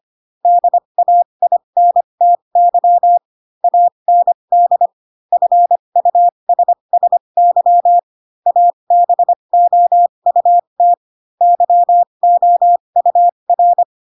Morse code, heard as a keyed tone, about 25 wpm, spaced as for 22 wpm, 700 Hz. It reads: DAINTY AND FUSSY ABOUT YOUR
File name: Pinocchio / Pinocchio ch7 18 25 ik1hge